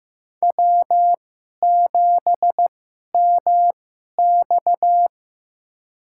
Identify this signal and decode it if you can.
Morse code, heard as a keyed tone, about 16 wpm, spaced as for 11 wpm, 700 Hz. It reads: W7MX
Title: USA Call Signs - 4 Characters - 2x Character Spacing - 15wpm